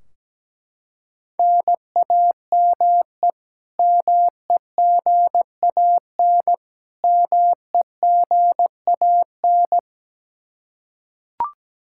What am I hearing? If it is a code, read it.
Morse code, heard as a keyed tone, about 17 wpm, 700 Hz. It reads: NAME MEGAN MEGAN
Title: Practice QSO - Exchange 1 - Beginning - 17wpm